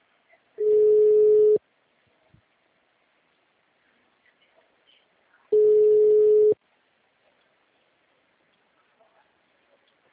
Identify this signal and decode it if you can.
Ringback tone (Europe, 425 Hz).